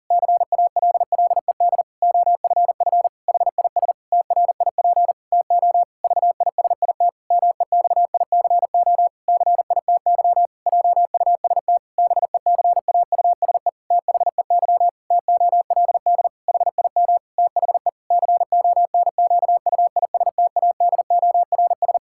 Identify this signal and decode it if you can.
Morse code, heard as a keyed tone, about 40 wpm, 700 Hz. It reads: CALLED OFF HIS TRIP TO VISIT MEXICO CITY JUST BECAUSE THEY TOLD HIM THE CONQUISTADORS
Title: Pangram Challenge v2 - 40wpm